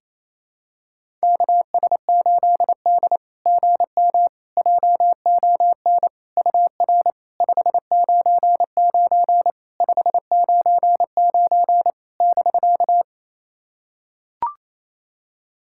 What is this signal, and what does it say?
Morse code, heard as a keyed tone, about 28 wpm, 690 Hz. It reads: KS8D GM JON UR 599 599 <BK>
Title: SOTA QSO Practice 28wpm